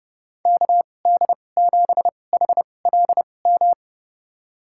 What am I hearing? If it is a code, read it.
Morse code, heard as a keyed tone, about 30 wpm, spaced as for 22 wpm, 700 Hz. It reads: KD7HLM
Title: USA Call Signs - 6 Characters - 2x Character Spacing - 30wpm